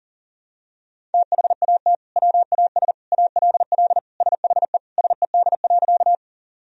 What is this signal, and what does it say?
Morse code, heard as a keyed tone, about 40 wpm, 700 Hz. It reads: THAT WAS ALL SHE SED.
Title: Sentences from Top 100 Words and Abbreviations 40wpm